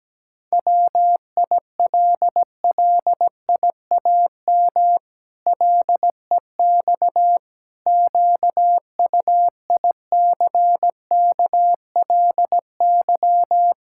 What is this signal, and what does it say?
Morse code, heard as a keyed tone, about 17 wpm, 700 Hz. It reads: WILLIAM LEX QUICKLY
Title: Pangram Challenge v2 - 17wpm